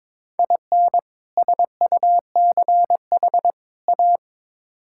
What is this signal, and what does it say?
Morse code, heard as a keyed tone, about 22 wpm, 700 Hz. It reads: IN SUCH A
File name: practice / 22wpm / Build A Phrase - 5-Word Phrases - 22wpm